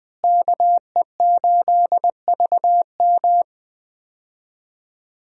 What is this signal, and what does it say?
Morse code, heard as a keyed tone, about 21 wpm, 700 Hz. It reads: KE8VM
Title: USA Call Signs - 5 Characters - Repeated 3x - 20wpm